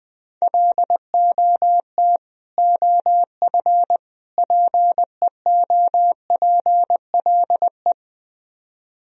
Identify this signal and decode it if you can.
Morse code, heard as a keyed tone, about 20 wpm, 700 Hz. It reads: LOT OF PEOPLE